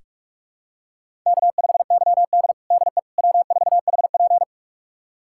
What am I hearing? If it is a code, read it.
Morse code, heard as a keyed tone, about 46 wpm, 700 Hz. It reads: K5YD DE W4HP
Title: USA Callsign DE Callsign - 4 Characters - 45wpm